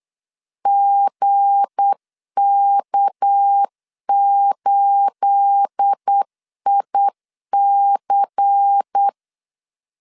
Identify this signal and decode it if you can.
Morse code, heard as a keyed tone, about 8 wpm, 800 Hz. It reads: GK8IC
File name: CW / DET / 1 9 1500C S C 8wpm 5653